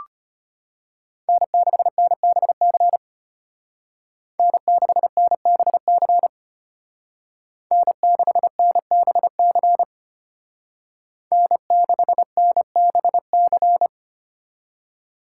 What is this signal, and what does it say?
Morse code, heard as a keyed tone, about 30 wpm, 700 Hz. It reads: N6NBC N6NBC N6NBC N6NBC
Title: Speed Racing - USA Call Signs 25wpm